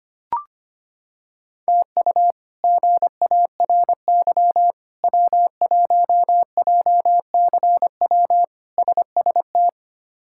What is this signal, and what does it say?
Morse code, heard as a keyed tone, about 25 wpm, 700 Hz. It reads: TU GARY W1JCW SST
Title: K1USN SST QSO Practice - 25wpm